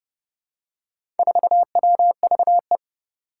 Morse code, heard as a keyed tone, about 30 wpm, 700 Hz. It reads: 4WVE
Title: Four Challenging Letter-Number Combinations 30wpm